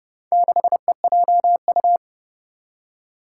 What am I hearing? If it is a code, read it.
Morse code, heard as a keyed tone, about 30 wpm, 700 Hz. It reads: 6EJU
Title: Four Challenging Letter-Number Combinations 30wpm